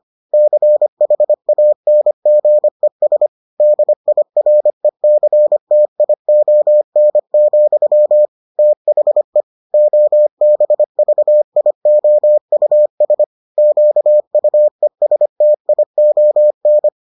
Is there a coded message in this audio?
Yes: Morse code from 0.3 to 16.9 s, heard as a keyed tone, about 25 wpm, 600 Hz. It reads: CHANGES DIRECTION, THE OBVIOUS QUESTION